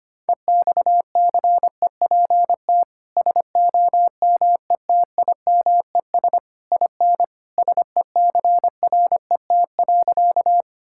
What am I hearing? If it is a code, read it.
Morse code, heard as a keyed tone, about 25 wpm, 700 Hz. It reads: EXCEPT SOMETIMES IN SECRET.
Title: Sentences from Top 1600 Words 25wpm